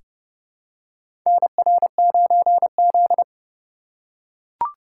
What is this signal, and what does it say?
Morse code, heard as a keyed tone, about 30 wpm, 700 Hz. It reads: NR9Z
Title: USA Call Signs - 4 Characters - 2x Character Spacing - 30wpm